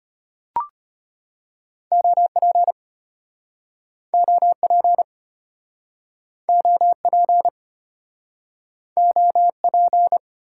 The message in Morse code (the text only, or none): OP OP OP OP